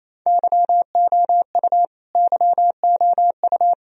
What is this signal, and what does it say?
Morse code, heard as a keyed tone, about 28 wpm, 700 Hz. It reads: YOU YOU
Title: Top 100 Words Repeated 28wpm